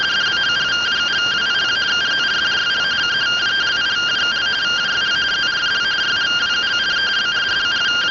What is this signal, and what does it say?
Two-tone teleprinter signal, about 170 Hz shift.